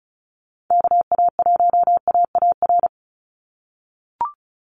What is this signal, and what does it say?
Morse code, heard as a keyed tone, about 34 wpm, 700 Hz. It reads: KA1AAR
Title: USA Call Signs 35wpm